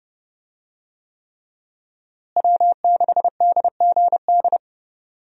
Morse code, heard as a keyed tone, about 32 wpm, 700 Hz. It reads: W6DGD
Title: USA Call Signs - 5 Characters - Repeated 3x - 30wpm